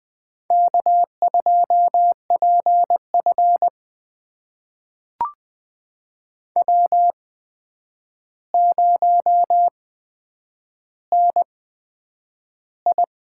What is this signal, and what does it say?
Morse code, heard as a keyed tone, about 20 wpm, spaced as for 9 wpm, 700 Hz. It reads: K2PF W0NI
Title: USA Call Signs - 4 Characters - 8x Character Spacing - 20wpm